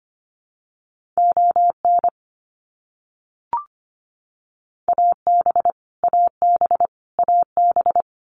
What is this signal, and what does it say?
Morse code, heard as a keyed tone, about 25 wpm, 700 Hz. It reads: ON AB AB AB
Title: Canadian Provinces and Territories - Repeated 3x - 25wpm